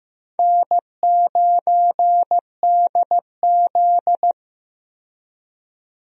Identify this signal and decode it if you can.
Morse code, heard as a keyed tone, about 15 wpm, 700 Hz. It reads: N9DZ
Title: USA Call Signs - 4 Characters - 15wpm